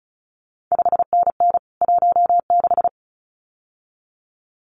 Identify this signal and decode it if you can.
Morse code, heard as a keyed tone, about 36 wpm, 700 Hz. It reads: 5NN 16
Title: CQ WW DX Contest Exchange - 35wpm